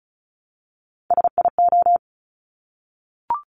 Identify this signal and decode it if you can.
Morse code, heard as a keyed tone, about 36 wpm, 700 Hz. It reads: SIO
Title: Top Three Letter Combinations in Words 35wpm